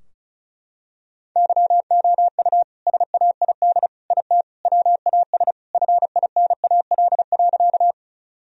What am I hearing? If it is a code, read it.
Morse code, heard as a keyed tone, about 36 wpm, 700 Hz. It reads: YOU SAID IT WAS FINAL.
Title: Sentences from Top 1000 Words - Review Part 2 - 35wpm